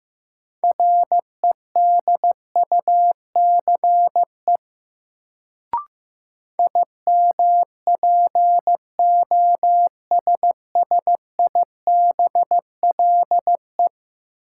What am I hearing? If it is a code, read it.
Morse code, heard as a keyed tone, about 15 wpm, 700 Hz. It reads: REDUCE IMPOSSIBLE
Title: Sentences from Top 1100 Words - New Word Drill Alt - 15wpm